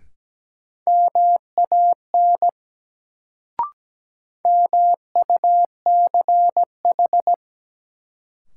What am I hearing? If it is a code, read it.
Morse code, heard as a keyed tone, about 17 wpm, 700 Hz. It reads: MAN MUCH
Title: Sets of 2 Words - New Word Drill - 17wpm